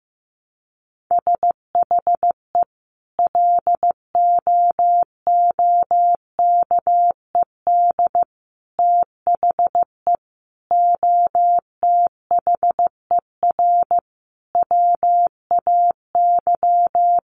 Morse code, heard as a keyed tone, about 15 wpm, 700 Hz. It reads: SHE LOOKED THE OTHER WAY
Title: Sentences from Top 800 Words 15wpm